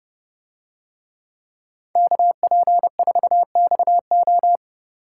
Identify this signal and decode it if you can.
Morse code, heard as a keyed tone, about 29 wpm, 700 Hz. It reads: KP4XO